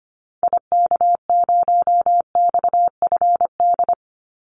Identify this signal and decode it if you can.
Morse code, heard as a keyed tone, about 25 wpm, 700 Hz. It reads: IK0XFD
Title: Call Signs Repeated 3x - 25wpm